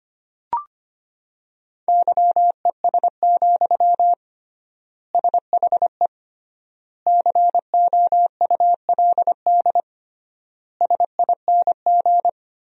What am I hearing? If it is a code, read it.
Morse code, heard as a keyed tone, about 25 wpm, 700 Hz. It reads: YES, SHE COULD SING
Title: 3x Word-Spacing - Sentences from Top 500 Words - 25wpm